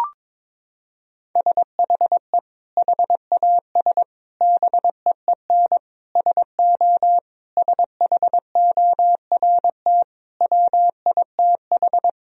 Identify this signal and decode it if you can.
Morse code, heard as a keyed tone, about 22 wpm, 700 Hz. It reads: SHE HAS BEEN SO SHORT WITH
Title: Sentences from Top 1000 Words - Review Part 2 - 22wpm